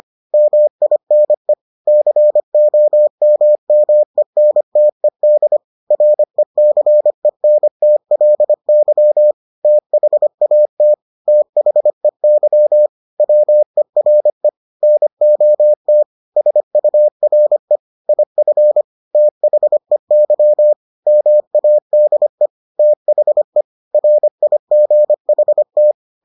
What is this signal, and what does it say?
Morse code, heard as a keyed tone, about 25 wpm, 600 Hz. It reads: MINE COMMENTED RECENTLY THAT THEY WERE NOT SURE IF THEY MADE THE RIGHT